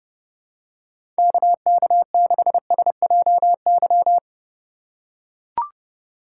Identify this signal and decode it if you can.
Morse code, heard as a keyed tone, about 30 wpm, 700 Hz. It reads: KK6SJY
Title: USA Call Signs - 3x Character Spacing - 30wpm